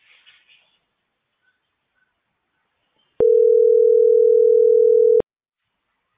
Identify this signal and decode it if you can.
Ringback tone (North America), 440 + 480 Hz.